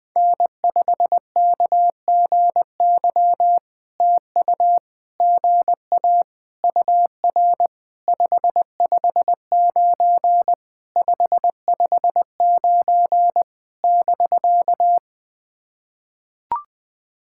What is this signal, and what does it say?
Morse code, heard as a keyed tone, about 20 wpm, 700 Hz. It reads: N5KGY TU GA UR 559 559 <BK>
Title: POTA QSO Practice - 20wpm